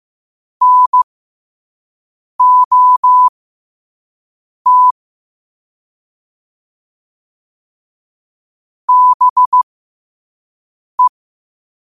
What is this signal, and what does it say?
Morse code, heard as a keyed tone, about 15 wpm, spaced as for 5 wpm, 1000 Hz. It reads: NOT BE
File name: qotdfiles / 5wpm